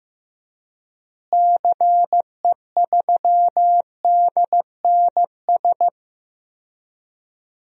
Morse code, heard as a keyed tone, about 16 wpm, 700 Hz. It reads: CE3DNS
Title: DXCC-100 Call Signs - 6 Characters - Repeated 2x - 15wpm